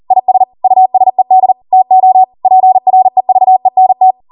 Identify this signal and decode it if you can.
Morse code, heard as a keyed tone, about 40 wpm, 750 Hz. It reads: IS USED TO PREVENT
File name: Morse / 40WPM